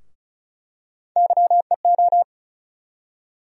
Morse code, heard as a keyed tone, about 34 wpm, 700 Hz. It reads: YEO
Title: USA Call Sign Postfixes - 3 Letters 35wpm